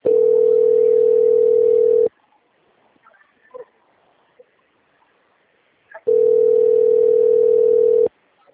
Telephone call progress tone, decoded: ringback tone

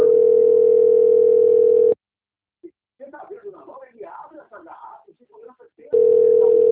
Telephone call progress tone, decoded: ringback tone

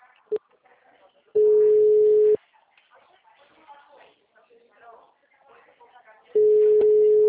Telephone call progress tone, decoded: ringback tone

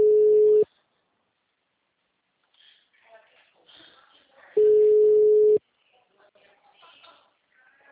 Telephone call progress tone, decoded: ringback tone

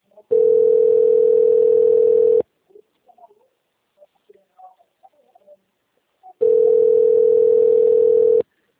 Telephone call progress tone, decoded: ringback tone

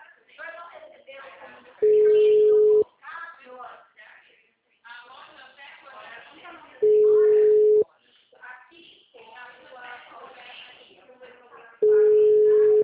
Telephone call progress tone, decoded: ringback tone

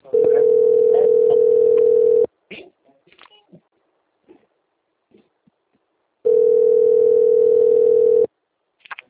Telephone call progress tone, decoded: ringback tone

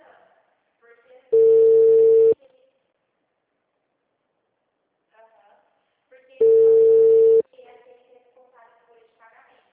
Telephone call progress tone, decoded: ringback tone

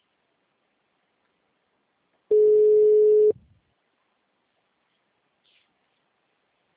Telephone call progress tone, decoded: ringback tone

